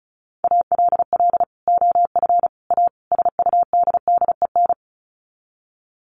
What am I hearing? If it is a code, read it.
Morse code, heard as a keyed tone, about 34 wpm, 700 Hz. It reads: ALL OF A SUDDEN
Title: Sets of 4 Words 35wpm